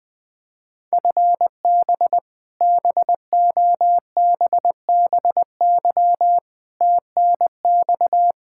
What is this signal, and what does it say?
Morse code, heard as a keyed tone, about 20 wpm, 700 Hz. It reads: FB BOBBY TNX
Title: Practice QSO - Exchange 3 - More Info - 20wpm